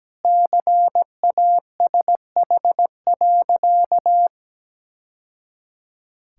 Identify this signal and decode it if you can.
Morse code, heard as a keyed tone, about 17 wpm, 700 Hz. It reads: CASH.
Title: Sentences from Top 1000 Words - Review Part 5 - 17wpm